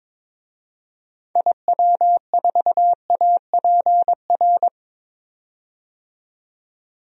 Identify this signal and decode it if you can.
Morse code, heard as a keyed tone, about 22 wpm, 700 Hz. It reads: IW4APR
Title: DXCC-100 Call Signs - 6 Characters - 22wpm